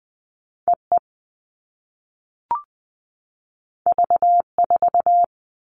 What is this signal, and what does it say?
Morse code, heard as a keyed tone, about 20 wpm, 700 Hz. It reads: EE V4
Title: All Cut Numbers - 2-Digit 20wpm